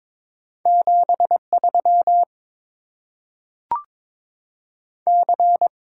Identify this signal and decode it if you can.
Morse code, heard as a keyed tone, about 22 wpm, 700 Hz. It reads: 73 C
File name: Top 20 QSO Elements - 8x Character Spacing - 22wpm